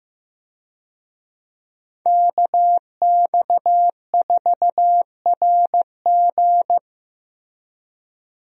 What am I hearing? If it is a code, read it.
Morse code, heard as a keyed tone, about 15 wpm, 700 Hz. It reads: KX4RG
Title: USA Call Signs - 5 Characters - Repeated 2x - 15wpm